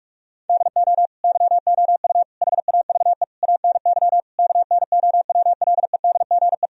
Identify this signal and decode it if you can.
Morse code, heard as a keyed tone, about 46 wpm, 700 Hz. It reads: DO YOU HAVE ANY KNOWLEDGE
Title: Sentences from Top 1000 Words - Review Part 3 - 45wpm